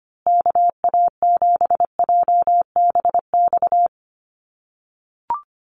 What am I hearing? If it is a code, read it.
Morse code, heard as a keyed tone, about 25 wpm, 700 Hz. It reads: KA7JBX
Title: USA Call Signs Repeated 3x 25wpm